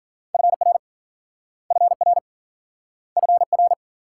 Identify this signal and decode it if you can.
Morse code, heard as a keyed tone, about 53 wpm, 700 Hz. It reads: FR FR FR